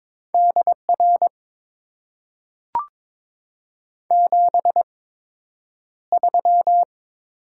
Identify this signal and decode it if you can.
Morse code, heard as a keyed tone, about 22 wpm, spaced as for 9 wpm, 700 Hz. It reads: DR 73
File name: Top 90 QSO Elements - 8x Character Spacing - 22wpm